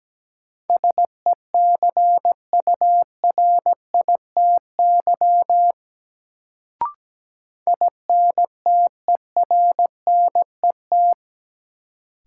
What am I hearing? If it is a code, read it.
Morse code, heard as a keyed tone, about 17 wpm, 700 Hz. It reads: SECURITY INTERNET
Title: Sentences from Top 1000 Words - Review Part 4 - New Word Drill - 17wpm